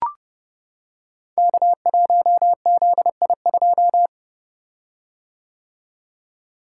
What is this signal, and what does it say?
Morse code, heard as a keyed tone, about 30 wpm, 700 Hz. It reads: K1ZI2
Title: Code Groups - 5 Letter:Numbers - 30wpm